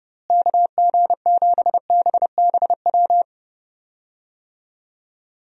Morse code, heard as a keyed tone, about 32 wpm, 700 Hz. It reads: KG7BBW
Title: USA Call Signs - 6 Characters - Repeated 3x - 30wpm